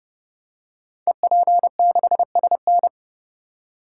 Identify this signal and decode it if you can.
Morse code, heard as a keyed tone, about 30 wpm, 700 Hz. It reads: EP6SN
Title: Code Groups - 5 Letter:Numbers - 30wpm